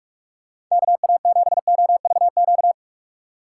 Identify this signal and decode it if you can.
Morse code, heard as a keyed tone, about 47 wpm, 700 Hz. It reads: KA7OVQ
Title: USA Call Signs - 6 Characters - 6x Character Spacing - 45wpm